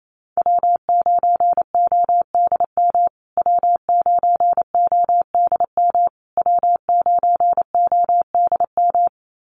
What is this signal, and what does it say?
Morse code, heard as a keyed tone, about 28 wpm, 700 Hz. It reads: W9ODM W9ODM W9ODM
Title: USA Call Signs Repeated 3x 28wpm